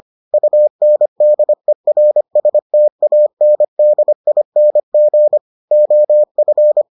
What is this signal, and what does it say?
Morse code, heard as a keyed tone, about 25 wpm, 600 Hz. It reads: UNDERSTANDING OF